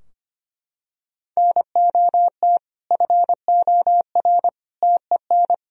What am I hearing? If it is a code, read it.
Morse code, heard as a keyed tone, about 25 wpm, 700 Hz. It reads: NOT FOR TEN
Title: Sentences from Top 1000 Words - Review Part 3 - 25wpm